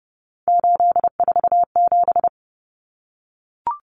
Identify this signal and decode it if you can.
Morse code, heard as a keyed tone, about 30 wpm, 700 Hz. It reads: 847
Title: Numbers - 3-Digit 30wpm